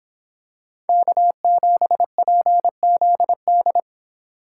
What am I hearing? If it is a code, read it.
Morse code, heard as a keyed tone, about 26 wpm, 700 Hz. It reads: K7PZD